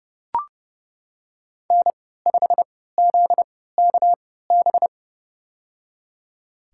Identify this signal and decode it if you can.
Morse code, heard as a keyed tone, about 31 wpm, spaced as for 17 wpm, 700 Hz. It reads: N5ZKB
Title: USA Call Signs - 5 Characters - 3x Character Spacing - 30wpm